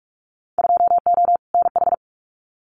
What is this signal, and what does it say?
Morse code, heard as a keyed tone, about 44 wpm, 700 Hz. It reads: 2O NH
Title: Winter Field Day Exchanges 45wpm